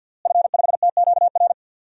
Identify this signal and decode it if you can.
Morse code, heard as a keyed tone, about 50 wpm, 700 Hz. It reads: F5TQR
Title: DXCC-100 Call Signs - 5 Characters - Repeated 3x - 50wpm